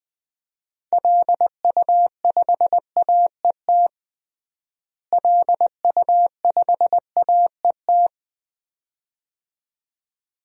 Morse code, heard as a keyed tone, about 20 wpm, 700 Hz. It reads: LU5AET LU5AET
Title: DXCC-100 Call Signs - 6 Characters - Repeated 2x - 20wpm